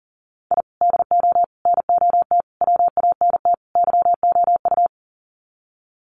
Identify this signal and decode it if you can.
Morse code, heard as a keyed tone, about 40 wpm, 700 Hz. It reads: I DO NOT WANT YOU
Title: Sets of 5 Words 40wpm